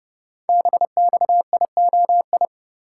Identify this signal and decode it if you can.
Morse code, heard as a keyed tone, about 28 wpm, 700 Hz. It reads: BXIOI
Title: Code Groups - 5 Letters - 30wpm